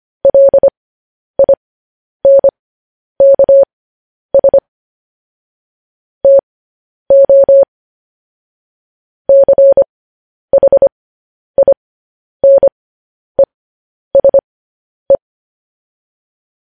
Morse code, heard as a keyed tone, about 25 wpm, spaced as for 10 wpm, 550 Hz. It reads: LINKS TO CHINESE
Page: news headlines in morse code at 10 wpm